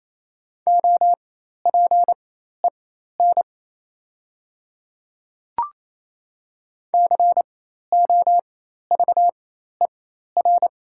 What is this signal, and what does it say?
Morse code, heard as a keyed tone, about 27 wpm, spaced as for 13 wpm, 700 Hz. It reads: OPEN COVER
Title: 4x Character Spacing - Top 300 Words - 28wpm